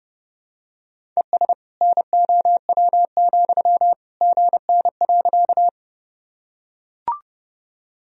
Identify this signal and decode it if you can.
Morse code, heard as a keyed tone, about 32 wpm, 700 Hz. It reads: ES NOW, GN.
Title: Sentences from Top 300 Words and Abbreviations 30wpm